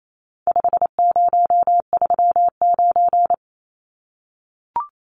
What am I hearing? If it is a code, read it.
Morse code, heard as a keyed tone, about 28 wpm, 700 Hz. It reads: 5039
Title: Numbers - 4-Digit 28wpm